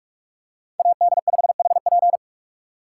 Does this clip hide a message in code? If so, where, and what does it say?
Yes: Morse code from 0.8 to 2.2 s, heard as a keyed tone, about 42 wpm, 700 Hz. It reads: AD5HP